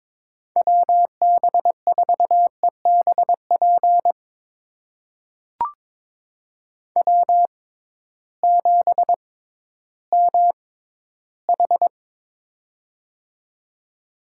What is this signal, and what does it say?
Morse code, heard as a keyed tone, about 22 wpm, spaced as for 13 wpm, 700 Hz. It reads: WB4EBP W7MH
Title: USA Call Signs - 6x Character Spacing - 22wpm